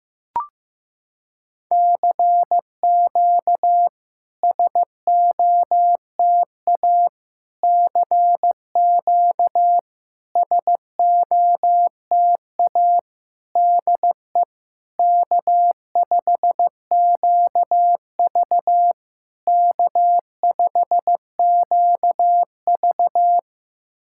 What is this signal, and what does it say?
Morse code, heard as a keyed tone, about 15 wpm, 700 Hz. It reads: CQ SOTA CQ SOTA DE K5QV K5QV
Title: SOTA QSO Practice 15wpm